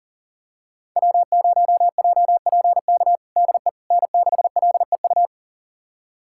Morse code, heard as a keyed tone, about 40 wpm, 700 Hz. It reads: W0JPK DE N6LEU
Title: USA Callsign DE Callsign - 5 Characters - 40wpm